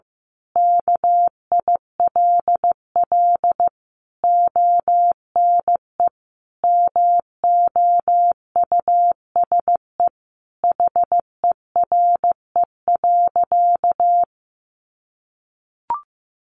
Morse code, heard as a keyed tone, about 15 wpm, 700 Hz. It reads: KILL ONE MOUSE HERE.
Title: A Sound Of Thunder 15wpm